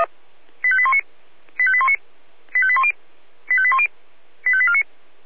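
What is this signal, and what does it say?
Five-tone selective call (ZVEI), heard as quick runs of five tones, tones of about 80 ms. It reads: 85819 85819 85810 85810 85849
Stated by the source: ZVEI five-tone selective call 85870, received by SDR.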